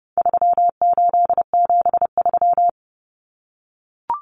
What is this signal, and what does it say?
Morse code, heard as a keyed tone, about 30 wpm, 700 Hz. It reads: 3873
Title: Numbers - 4-Digit 30wpm